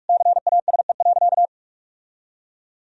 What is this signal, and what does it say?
Morse code, heard as a keyed tone, about 47 wpm, 700 Hz. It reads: CASE.
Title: Sentences from Top 1000 Words - Review Part 4 - 45wpm